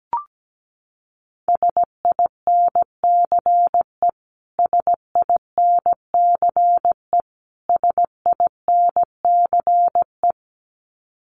Morse code, heard as a keyed tone, about 17 wpm, 700 Hz. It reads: SINCE SINCE SINCE
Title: Top 500 Words Repeated 17wpm